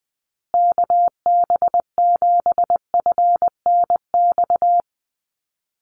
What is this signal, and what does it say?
Morse code, heard as a keyed tone, about 20 wpm, 700 Hz. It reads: KB7FNX
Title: USA Call Signs Repeated 2x 20wpm